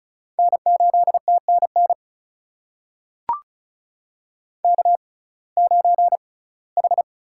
Morse code, heard as a keyed tone, about 36 wpm, spaced as for 21 wpm, 700 Hz. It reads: N8TNN K9H
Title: USA Call Signs - 5 Characters - 6x Character Spacing - 35wpm